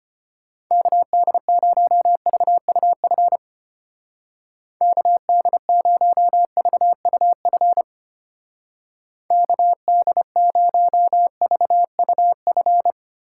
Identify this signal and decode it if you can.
Morse code, heard as a keyed tone, about 30 wpm, 700 Hz. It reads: KD0VUF KD0VUF KD0VUF
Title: Speed Racing - USA 2x3 Call Signs 25wpm